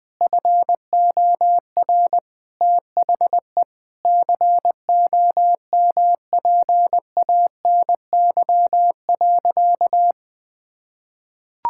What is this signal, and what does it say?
Morse code, heard as a keyed tone, about 20 wpm, 700 Hz. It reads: FOR THE COMPANY.
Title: Sentences with 2-Word Phrases - 20wpm